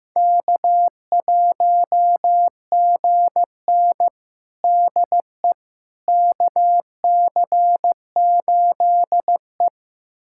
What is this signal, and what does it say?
Morse code, heard as a keyed tone, about 15 wpm, 700 Hz. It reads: K1GN DE KC8E
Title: USA Callsign DE Callsign - 4 Characters - 15wpm